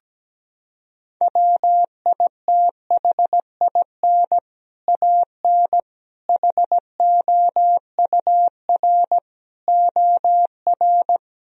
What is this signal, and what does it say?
Morse code, heard as a keyed tone, about 17 wpm, 700 Hz. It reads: WITHIN AN HOUR OR
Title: Sentences from Top 1000 Words - Review Part 4 - 17wpm